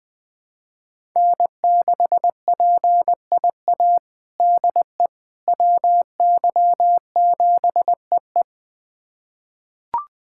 Morse code, heard as a keyed tone, about 20 wpm, 700 Hz. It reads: N6PIA DE WY7EE
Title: USA Callsign DE Callsign - 5 Characters - 20wpm